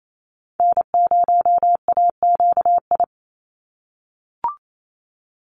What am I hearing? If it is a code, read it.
Morse code, heard as a keyed tone, about 28 wpm, 700 Hz. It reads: N0AQI
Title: USA Call Signs 28wpm